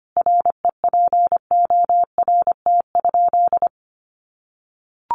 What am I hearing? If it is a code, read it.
Morse code, heard as a keyed tone, about 25 wpm, 700 Hz. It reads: REPORT?
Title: Sentences from Top 900 Words 25wpm